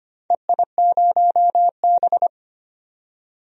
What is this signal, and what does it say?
Morse code, heard as a keyed tone, about 25 wpm, 700 Hz. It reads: EI0B
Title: DXCC-100 Call Signs - 4 Characters - 25wpm